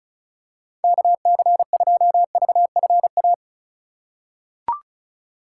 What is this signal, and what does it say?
Morse code, heard as a keyed tone, about 36 wpm, 700 Hz. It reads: KC2VFA
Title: USA Call Signs - 6 Characters - 3x Character Spacing - 35wpm